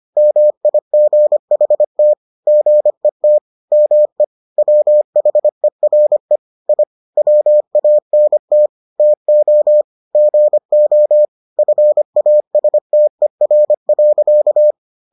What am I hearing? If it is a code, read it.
Morse code, heard as a keyed tone, about 25 wpm, 600 Hz. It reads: MIGHT GET ME WHERE I WANT TO GO FASTER.